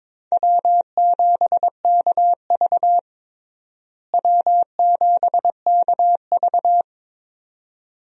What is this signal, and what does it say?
Morse code, heard as a keyed tone, about 22 wpm, 700 Hz. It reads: W7KV W7KV
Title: USA Call Signs - 4 Characters - Repeated 2x - 22wpm